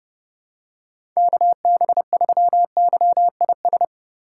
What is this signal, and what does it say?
Morse code, heard as a keyed tone, about 32 wpm, 700 Hz. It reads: KB3YIS
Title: USA Call Signs - 6 Characters - 6x Character Spacing - 30wpm